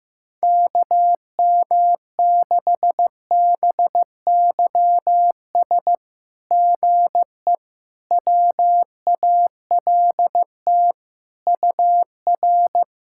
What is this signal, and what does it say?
Morse code, heard as a keyed tone, about 15 wpm, 700 Hz. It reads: KM6BYS GE WALT UR
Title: SOTA QSO Practice 15wpm